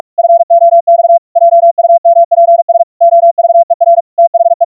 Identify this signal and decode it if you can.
Morse code, heard as a keyed tone, about 47 wpm, 680 Hz. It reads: FOX JUMPS OVER THE